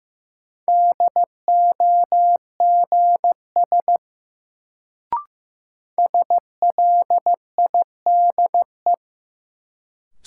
Morse code, heard as a keyed tone, about 15 wpm, 700 Hz. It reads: DOGS SLIDE
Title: Sentences from Top 1600 Words - New Word Drill Alt - 15wpm